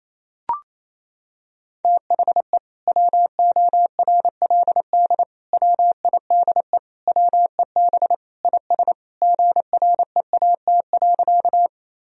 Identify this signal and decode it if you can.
Morse code, heard as a keyed tone, about 28 wpm, 700 Hz. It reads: THE WORLD WIDE WEB IS GREAT.
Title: Sentences from Top 1000 Words - Review Part 4 - 28wpm